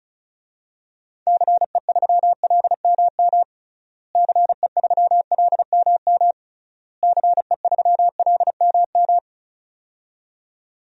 Morse code, heard as a keyed tone, about 34 wpm, 700 Hz. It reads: CE3LMM CE3LMM CE3LMM
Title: DXCC-100 Call Signs - 6 Characters - Repeated 3x - 35wpm